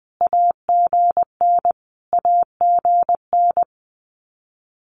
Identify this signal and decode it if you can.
Morse code, heard as a keyed tone, about 20 wpm, 700 Hz. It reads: AGN AGN
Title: Abbreviations Repeated 3x - 20wpm